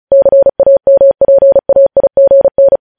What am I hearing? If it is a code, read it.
Morse code, heard as a keyed tone, about 36 wpm, 550 Hz. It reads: CAMPAIGN